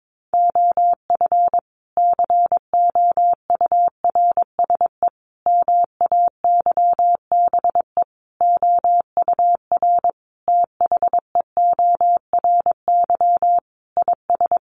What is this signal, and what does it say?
Morse code, heard as a keyed tone, about 22 wpm, 700 Hz. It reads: OF COURSE MAYBE OUR THEORY IS
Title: A Sound Of Thunder 22wpm